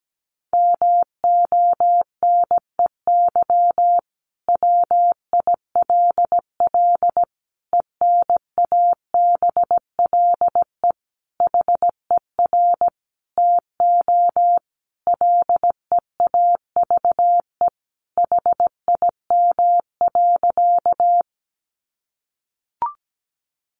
Morse code, heard as a keyed tone, about 17 wpm, 700 Hz. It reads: MONEY WILL ENABLE HER TO LEAVE HIM.
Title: Sentences from Top 1500 Words 17wpm